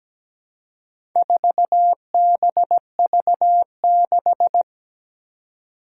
Morse code, heard as a keyed tone, about 17 wpm, 700 Hz. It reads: 4BV6
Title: Four Challenging Letter-Number Combinations 17wpm